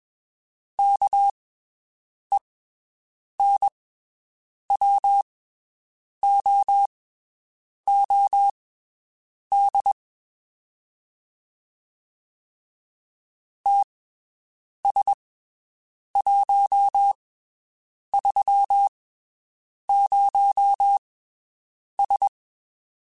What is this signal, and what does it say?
Morse code, heard as a keyed tone, about 21 wpm, spaced as for 7 wpm, 800 Hz. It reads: KENWOOD TS130S
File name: (spaced) / QSOs Spaced-mp3s / WD0BGP 1 @20 sp